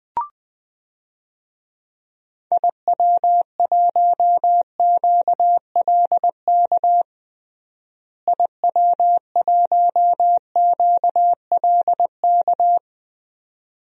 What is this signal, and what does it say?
Morse code, heard as a keyed tone, about 20 wpm, 700 Hz. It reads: IW1QLK IW1QLK
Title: DXCC-100 Call Signs - 6 Characters - Repeated 2x - 20wpm